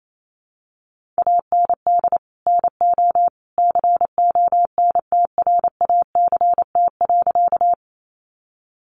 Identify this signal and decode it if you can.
Morse code, heard as a keyed tone, about 28 wpm, 700 Hz. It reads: AND NO CONTRACT.
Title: Sentences from Top 1400 Words 28wpm